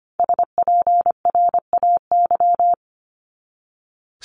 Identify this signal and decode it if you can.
Morse code, heard as a keyed tone, about 25 wpm, 700 Hz. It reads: SPRAY
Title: Top 1751-2000 Words 25wpm